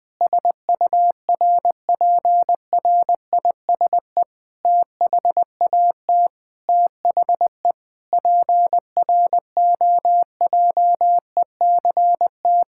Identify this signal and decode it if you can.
Morse code, heard as a keyed tone, about 20 wpm, 700 Hz. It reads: SURPRISE THAT THE PROJECT
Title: Sentences with 9-Word Phrases - Spoken Beginning - 20wpm